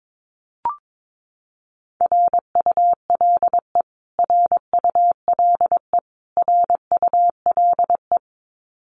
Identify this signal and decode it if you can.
Morse code, heard as a keyed tone, about 22 wpm, 700 Hz. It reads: RULE RULE RULE
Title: Top 500 Words Repeated 22wpm